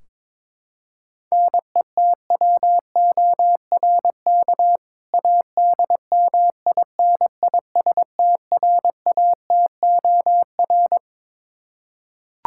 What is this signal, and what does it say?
Morse code, heard as a keyed tone, about 22 wpm, 700 Hz. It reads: NETWORK ADMINISTRATOR
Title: K1USN SST QSO Practice 22wpm